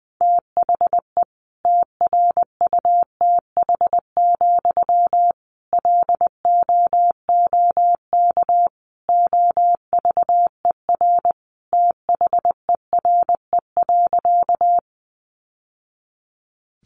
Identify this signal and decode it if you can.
Morse code, heard as a keyed tone, about 20 wpm, 700 Hz. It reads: THE TRUTH, LOOK OVER THERE.
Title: Sentences from Top 800 Words 20wpm